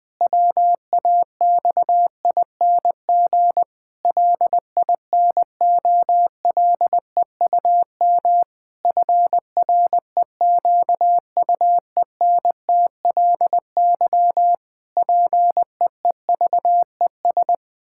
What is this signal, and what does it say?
Morse code, heard as a keyed tone, about 20 wpm, 700 Hz. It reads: WAXING LINOLEUM FREQUENTLY PEEVES
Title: Pangram Challenge v2 - 20wpm